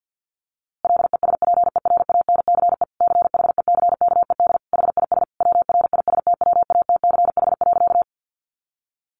Morse code, heard as a keyed tone, about 50 wpm, 700 Hz. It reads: LESPERANCE CHECKED HIS WRISTWATCH.
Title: A Sound Of Thunder 50wpm